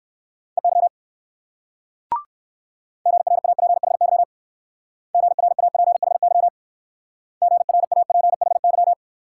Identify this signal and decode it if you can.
Morse code, heard as a keyed tone, about 59 wpm, 690 Hz. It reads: EY GRAPHY GRAPHY GRAPHY